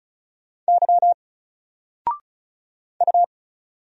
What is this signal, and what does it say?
Morse code, heard as a keyed tone, about 35 wpm, 700 Hz. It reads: Y U
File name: Single-Letter-Number-35wpm